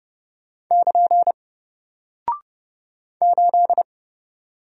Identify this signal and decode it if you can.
Morse code, heard as a keyed tone, about 30 wpm, 700 Hz. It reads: <KN> 8
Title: Single Character (Letter:Number:Punctuation:Prosign) Alt 30wpm